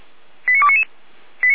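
Five-tone selective call (ZVEI), heard as quick runs of five tones, tones of about 80 ms. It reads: 89299
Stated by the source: ZVEI five-tone selective call received by SDR and recorded by the fire brigade (Feuerwehr) Orlamünde.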